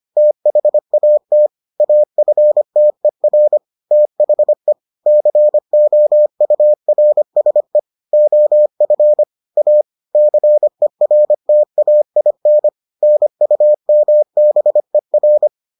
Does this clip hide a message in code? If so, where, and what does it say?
Yes: Morse code from 0.2 to 15.5 s, heard as a keyed tone, about 25 wpm, 600 Hz. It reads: THAT AFTER THE COURSE OF A CERTAIN NUMBER